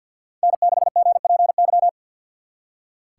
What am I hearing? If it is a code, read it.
Morse code, heard as a keyed tone, about 64 wpm, 700 Hz. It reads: N6GPY